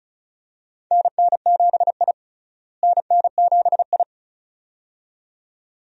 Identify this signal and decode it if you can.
Morse code, heard as a keyed tone, about 34 wpm, 700 Hz. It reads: NN7I NN7I
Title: USA Call Signs - 4 Characters - Repeated 2x - 35wpm